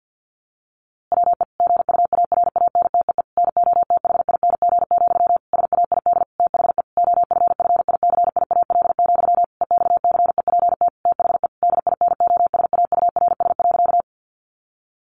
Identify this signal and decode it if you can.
Morse code, heard as a keyed tone, about 50 wpm, 700 Hz. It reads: WE GUARANTEE NOTHING, SAID THE OFFICIAL, EXCEPT THE DINOSAURS.
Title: A Sound Of Thunder 50wpm